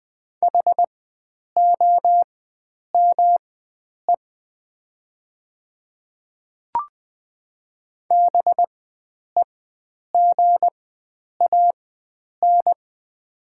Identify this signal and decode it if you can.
Morse code, heard as a keyed tone, about 21 wpm, spaced as for 9 wpm, 700 Hz. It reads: HOME BEGAN
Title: 4x Character Spacing - Top 300 Words - 20wpm